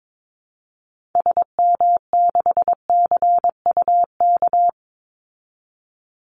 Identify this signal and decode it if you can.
Morse code, heard as a keyed tone, about 22 wpm, 700 Hz. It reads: SM6CUK
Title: Call Signs Repeated 2x - 22wpm